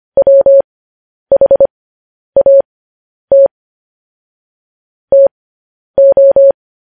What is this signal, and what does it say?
Morse code, heard as a keyed tone, about 25 wpm, spaced as for 10 wpm, 550 Hz. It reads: WHAT TO
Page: news headlines in morse code at 10 wpm